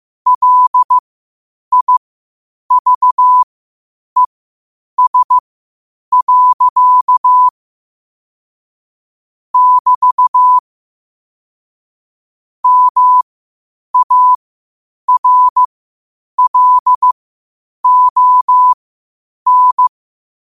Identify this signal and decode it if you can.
Morse code, heard as a keyed tone, about 15 wpm, spaced as for 8 wpm, 1000 Hz. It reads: LIVES. <BT> MARLON
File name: qotdfiles / 8wpm